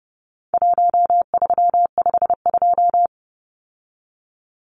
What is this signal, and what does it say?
Morse code, heard as a keyed tone, about 30 wpm, 700 Hz. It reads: 1352
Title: Numbers - Contest Style - 4-Digit 30wpm